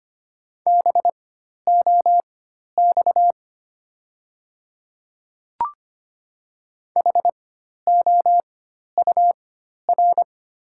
Morse code, heard as a keyed tone, about 25 wpm, spaced as for 12 wpm, 690 Hz. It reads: BOX HOUR
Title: 4x Character Spacing - Top 500 Words - 25wpm